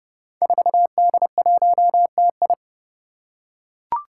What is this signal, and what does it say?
Morse code, heard as a keyed tone, about 30 wpm, 700 Hz. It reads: 4D1TI
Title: Five Challenging Letter-Number Combinations 30wpm